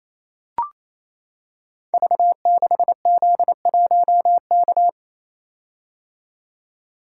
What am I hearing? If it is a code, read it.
Morse code, heard as a keyed tone, about 27 wpm, 700 Hz. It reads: V6Z1K
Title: Code Groups - 5 Letter:Numbers - 28wpm